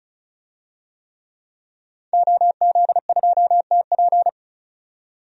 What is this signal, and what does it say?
Morse code, heard as a keyed tone, about 36 wpm, 700 Hz. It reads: OZ2TP
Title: DXCC-100 Call Signs - 5 Characters - 35wpm